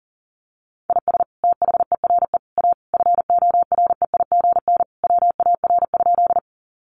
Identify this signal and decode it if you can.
Morse code, heard as a keyed tone, about 40 wpm, 700 Hz. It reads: IS THERE A FOREIGN WAR?
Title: Sentences from Top 1400 Words 40wpm